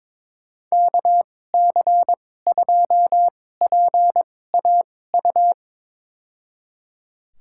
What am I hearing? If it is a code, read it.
Morse code, heard as a keyed tone, about 22 wpm, spaced as for 16 wpm, 700 Hz. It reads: KC2PAU
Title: USA Call Signs - 6 Characters - 2x Character Spacing - 22wpm